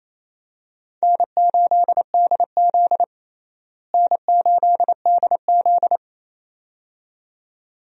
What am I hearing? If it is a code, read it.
Morse code, heard as a keyed tone, about 27 wpm, 700 Hz. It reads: N8DZ N8DZ
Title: USA Call Signs - 4 Characters - Repeated 2x - 28wpm